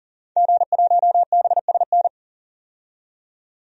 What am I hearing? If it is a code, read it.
Morse code, heard as a keyed tone, about 40 wpm, 700 Hz. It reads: G1BSN
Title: DXCC-100 Call Signs - 5 Characters - 40wpm